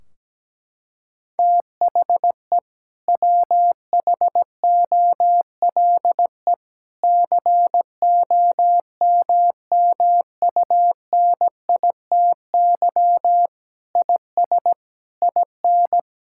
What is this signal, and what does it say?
Morse code, heard as a keyed tone, about 17 wpm, 700 Hz. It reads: THE WHOLE COMMUNITY IS IN
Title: Sentences from Top 1600 Words 17wpm